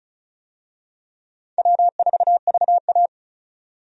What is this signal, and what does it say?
Morse code, heard as a keyed tone, about 36 wpm, 700 Hz. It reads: W4VA
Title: USA Call Signs - 4 Characters - 35wpm